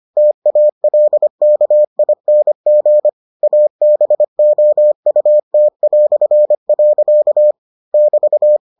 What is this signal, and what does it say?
Morse code, heard as a keyed tone, about 25 wpm, 600 Hz. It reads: TALKING ABOUT". <BT>